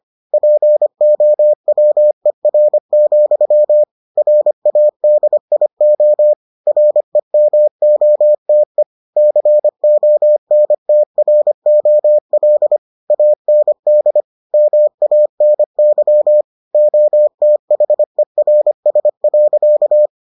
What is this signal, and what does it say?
Morse code, heard as a keyed tone, about 25 wpm, 600 Hz. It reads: POWER, RADIO REMOTE CONTROL AND MANY OTHERS.